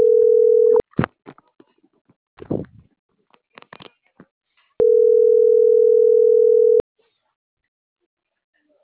Ringback tone (North America), 440 + 480 Hz.